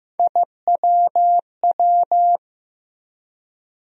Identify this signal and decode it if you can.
Morse code, heard as a keyed tone, about 15 wpm, 700 Hz. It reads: IWW
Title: USA Call Sign Postfixes - 3 Letters 15wpm